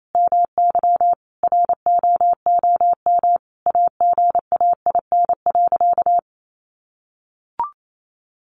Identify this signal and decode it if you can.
Morse code, heard as a keyed tone, about 28 wpm, 700 Hz. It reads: MY ROOM AGAIN.
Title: Sentences from Top 1300 Words 28wpm